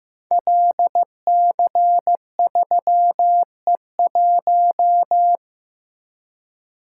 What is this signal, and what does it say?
Morse code, heard as a keyed tone, about 15 wpm, 700 Hz. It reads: LC3E1
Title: Code Groups - 5 Letter:Numbers - 15wpm